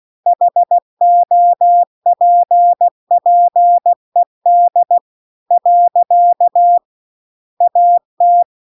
Morse code, heard as a keyed tone, about 16 wpm, 700 Hz. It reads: HOPPED . AT
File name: CW-The-Big-Twoheaded-River-2-16WPM